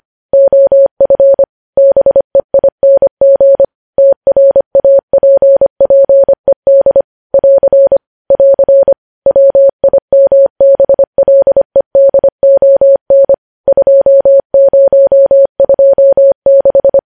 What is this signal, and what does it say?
Morse code, heard as a keyed tone, about 25 wpm, 550 Hz. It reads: OF BEING TRAPPED <AR> <AR> WIMBLEDON 2026
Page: news-headlines-in-morse-code-at-25-wpm